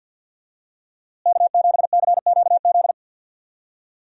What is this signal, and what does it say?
Morse code, heard as a keyed tone, about 50 wpm, 700 Hz. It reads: K7CQZ
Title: USA Call Signs - 5 Characters - Repeated 3x - 50wpm